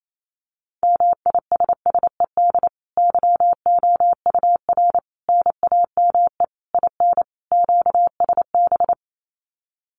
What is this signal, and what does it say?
Morse code, heard as a keyed tone, about 28 wpm, 700 Hz. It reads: MISSED YOUR NAME IN QSB
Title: Common QSO Phrases 28wpm